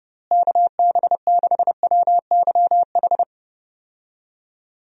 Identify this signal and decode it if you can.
Morse code, heard as a keyed tone, about 32 wpm, 700 Hz. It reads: KB6WYH
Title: USA Call Signs - 6 Characters - Repeated 3x - 30wpm